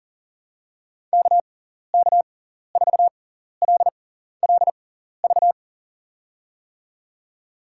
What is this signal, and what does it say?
Morse code, heard as a keyed tone, about 40 wpm, spaced as for 14 wpm, 700 Hz. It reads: KK4LLV
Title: USA Call Signs - 6 Characters - 6x Character Spacing - 40wpm